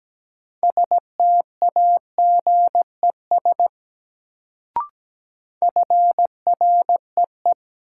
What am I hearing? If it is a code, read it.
Morse code, heard as a keyed tone, about 17 wpm, 700 Hz. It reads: STAGES FREE
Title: Sets of 5 Words Encore - New Word Drill - 17wpm